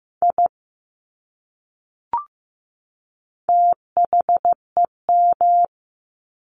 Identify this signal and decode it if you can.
Morse code, heard as a keyed tone, about 15 wpm, 700 Hz. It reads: I THEM
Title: Top 200 Words 15wpm